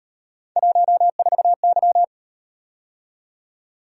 Morse code, heard as a keyed tone, about 37 wpm, 700 Hz. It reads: 14Y